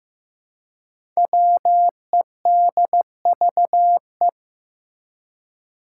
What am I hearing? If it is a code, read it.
Morse code, heard as a keyed tone, about 15 wpm, 700 Hz. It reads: WEDVE
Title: Five Challenging Letter-Number Combinations 15wpm